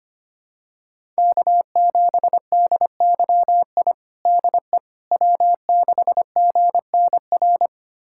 Morse code, heard as a keyed tone, about 25 wpm, 700 Hz. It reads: K7DYI DE W6GNR
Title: USA Callsign DE Callsign - 5 Characters - 25wpm